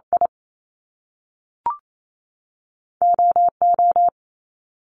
Morse code, heard as a keyed tone, about 28 wpm, 700 Hz. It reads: I OO
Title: Top Two Letter Combinations in Words 28wpm